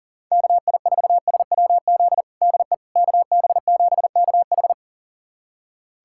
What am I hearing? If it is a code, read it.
Morse code, heard as a keyed tone, about 40 wpm, 700 Hz. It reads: KI4SWZ DE KB7KH
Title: USA Callsign DE Callsign 40wpm